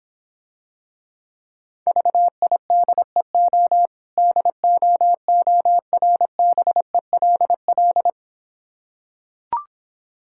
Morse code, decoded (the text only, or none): VIDEO DOORBELL